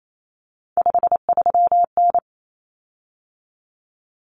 Morse code, heard as a keyed tone, about 28 wpm, 700 Hz. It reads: 53N
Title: Common Cut Numbers - 3-Digit 28wpm